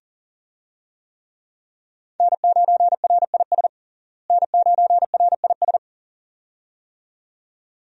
Morse code, heard as a keyed tone, about 37 wpm, 700 Hz. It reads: N9RIS N9RIS
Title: USA Call Signs - 5 Characters - Repeated 2x - 40wpm